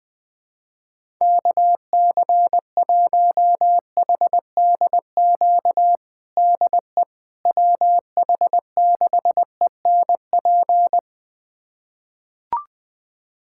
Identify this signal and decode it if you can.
Morse code, heard as a keyed tone, about 20 wpm, 700 Hz. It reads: KC1HDQ DE WH6ENP
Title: USA Callsign DE Callsign - 6 Characters - 20wpm